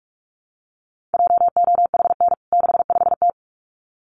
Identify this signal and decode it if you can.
Morse code, heard as a keyed tone, about 44 wpm, 700 Hz. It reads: JOHN 65T
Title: CWT Exchange 45wpm